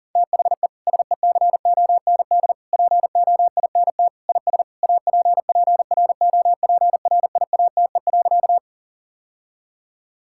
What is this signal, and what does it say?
Morse code, heard as a keyed tone, about 40 wpm, 700 Hz. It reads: THE SECOND POINT IS APPROPRIATE.
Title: Sentences from Top 1000 Words - Review Part 5 - 40wpm